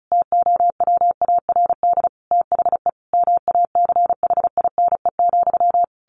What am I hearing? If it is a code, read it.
Morse code, heard as a keyed tone, about 34 wpm, 700 Hz. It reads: TOWARD THE MACHINE,
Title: A Sound Of Thunder 35wpm